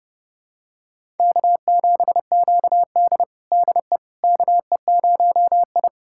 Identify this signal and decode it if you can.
Morse code, heard as a keyed tone, about 30 wpm, 700 Hz. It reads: K7QD DE KE0I
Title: USA Callsign DE Callsign - 4 Characters - 30wpm